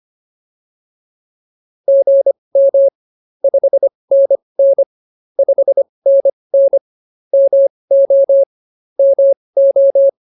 Morse code, heard as a keyed tone, about 25 wpm, spaced as for 20 wpm, 590 Hz. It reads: GM 5NN 5NN MO MO